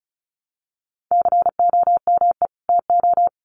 Morse code, heard as a keyed tone, about 34 wpm, 700 Hz. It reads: COME TO
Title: Sets of 2 Words 35wpm